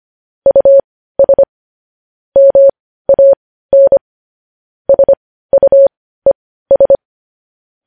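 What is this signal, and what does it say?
Morse code, heard as a keyed tone, about 25 wpm, spaced as for 15 wpm, 550 Hz. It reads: US MAN SUES
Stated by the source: Morse code news headlines at 15 wpm.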